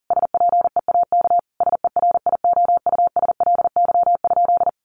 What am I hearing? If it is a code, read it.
Morse code, heard as a keyed tone, about 37 wpm, 700 Hz. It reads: SPEAK SERIOUSLY?
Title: Sentences from Top 1200 Words 40wpm